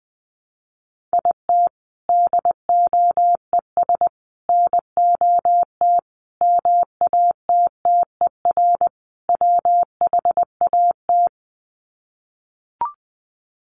Morse code, heard as a keyed tone, about 20 wpm, 700 Hz. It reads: IT DOES NOT MATTER WHAT
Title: Sets of 5 Words Encore 20wpm